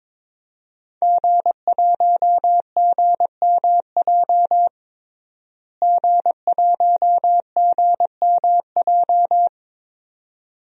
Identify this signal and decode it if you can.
Morse code, heard as a keyed tone, about 21 wpm, 700 Hz. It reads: G1GMJ G1GMJ
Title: DXCC-100 Call Signs - 5 Characters - Repeated 2x - 22wpm